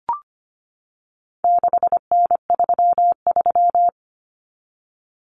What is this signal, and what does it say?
Morse code, heard as a keyed tone, about 25 wpm, 700 Hz. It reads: 6N33
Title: Common Cut Numbers - 4-Digit 25wpm